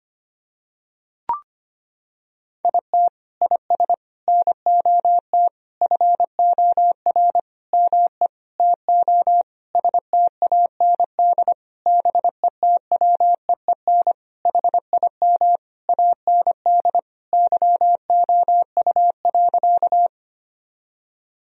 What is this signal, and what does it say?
Morse code, heard as a keyed tone, about 25 wpm, 700 Hz. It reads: IT IS NOT FOR ME TO STAND BETWEEN HIM AND YOU.
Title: Sentences from Top 1000 Words - Review Part 1 - 25wpm